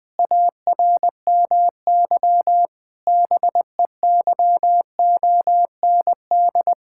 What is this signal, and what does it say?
Morse code, heard as a keyed tone, about 20 wpm, 700 Hz. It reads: ARMY BEYOND
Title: Sentences from Top 1000 Words - Review Part 6 - 20wpm